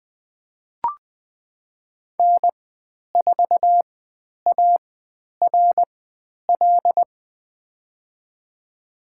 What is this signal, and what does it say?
Morse code, heard as a keyed tone, about 20 wpm, spaced as for 10 wpm, 700 Hz. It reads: N4ARL
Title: 095 Lesson 16g - Letter c - Call Signs 20wpm